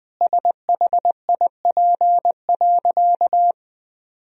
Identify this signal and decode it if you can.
Morse code, heard as a keyed tone, about 20 wpm, 700 Hz. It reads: SHIP.
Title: Sentences from Top 1000 Words - Review Part 2 - 20wpm